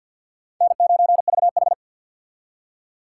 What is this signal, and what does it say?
Morse code, heard as a keyed tone, about 65 wpm, 700 Hz. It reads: N9VH